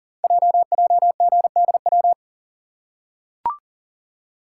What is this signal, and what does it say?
Morse code, heard as a keyed tone, about 43 wpm, 700 Hz. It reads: JJGDW
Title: Code Groups - 5 Letters - 40wpm